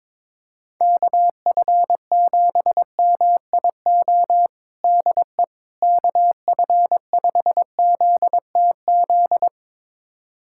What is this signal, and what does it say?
Morse code, heard as a keyed tone, about 22 wpm, 700 Hz. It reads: KF7MIO DE KF5ZTZ
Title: USA Callsign DE Callsign 22wpm